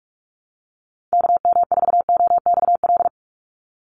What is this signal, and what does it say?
Morse code, heard as a keyed tone, about 44 wpm, 700 Hz. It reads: KM4OXL
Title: USA Call Signs 45wpm